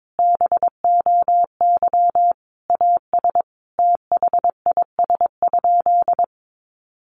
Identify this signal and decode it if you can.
Morse code, heard as a keyed tone, about 22 wpm, 700 Hz. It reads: BOY AS THIS?
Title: Sentences from Top 400 Words 22wpm